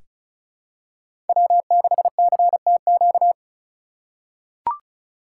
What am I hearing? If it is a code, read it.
Morse code, heard as a keyed tone, about 36 wpm, 700 Hz. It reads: W6CTQ
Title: USA Call Signs - 5 Characters - 3x Character Spacing - 35wpm